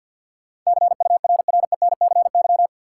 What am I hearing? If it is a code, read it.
Morse code, heard as a keyed tone, about 53 wpm, 700 Hz. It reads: CURRENCY
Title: Sentences from Top 1500 Words - New Word Drill - 50wpm